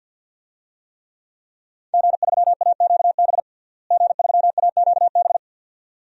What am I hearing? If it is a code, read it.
Morse code, heard as a keyed tone, about 53 wpm, 700 Hz. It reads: G3AQB G3AQB
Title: DXCC-100 Call Signs - 5 Characters - Repeated 2x - 50wpm